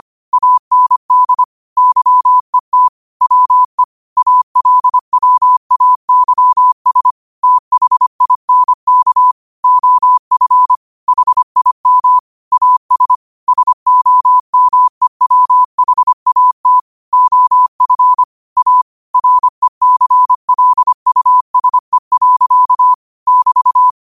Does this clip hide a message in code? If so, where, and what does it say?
Yes: Morse code from 0.3 to 23.9 s, heard as a keyed tone, about 25 wpm, 1000 Hz. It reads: AND YET WE ALWAYS THINK OF HIM AS SOMEWHAT OF A RECLUSE. <BT>